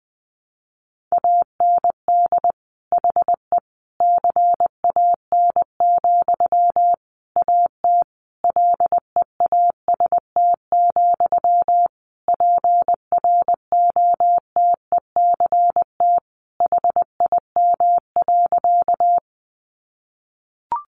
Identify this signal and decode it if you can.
Morse code, heard as a keyed tone, about 20 wpm, 700 Hz. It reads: AND HE CAN, AT LEAST, PROTECT HIM.
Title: Sentences from Top 900 Words 20wpm